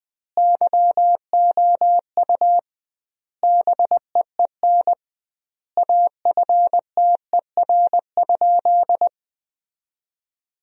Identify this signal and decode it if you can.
Morse code, heard as a keyed tone, about 20 wpm, 700 Hz. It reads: YOU BEEN AFTER?
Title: 2x Word-Spacing - Sentences from Top 200 Words - 20wpm